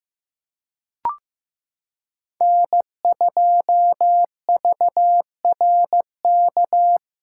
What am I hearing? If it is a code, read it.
Morse code, heard as a keyed tone, about 15 wpm, 700 Hz. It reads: N2VRK
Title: USA Callsign DE Callsign - 5 Characters - 15wpm